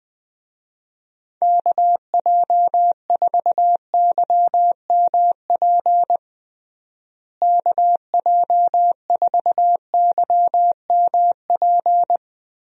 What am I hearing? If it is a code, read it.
Morse code, heard as a keyed tone, about 20 wpm, 700 Hz. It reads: KJ4YMP KJ4YMP
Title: USA Call Signs - 6 Characters - Repeated 3x - 20wpm